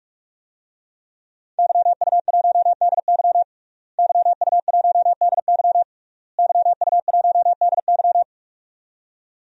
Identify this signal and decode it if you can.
Morse code, heard as a keyed tone, about 41 wpm, 700 Hz. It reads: YU1DY YU1DY YU1DY
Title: DXCC-100 Call Signs - 5 Characters - Repeated 3x - 45wpm